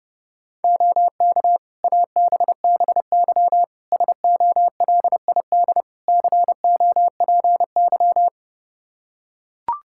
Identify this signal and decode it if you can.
Morse code, heard as a keyed tone, about 30 wpm, 700 Hz. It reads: OK ABBY SOLID COPY
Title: Practice QSO - Exchange 3 - More Info - 30wpm